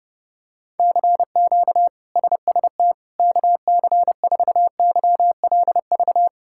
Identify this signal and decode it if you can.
Morse code, heard as a keyed tone, about 30 wpm, 700 Hz. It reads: CQ SST KC4YLV
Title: K1USN SST QSO Practice - 30wpm